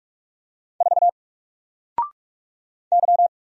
Morse code, heard as a keyed tone, about 44 wpm, 700 Hz. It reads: 4 Y
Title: Single Letter-Number Alt 45wpm